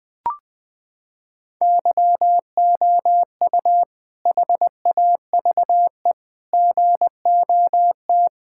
Morse code, heard as a keyed tone, about 20 wpm, 700 Hz. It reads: YOU HAVE GOT
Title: Sentences with 2-Word Phrases - 20wpm